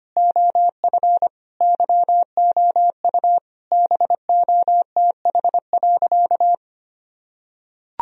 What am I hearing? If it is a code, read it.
Morse code, heard as a keyed tone, about 25 wpm, 700 Hz. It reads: OF YOU BOTH.
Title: Sentences from Top 1000 Words - Review Part 1 - 25wpm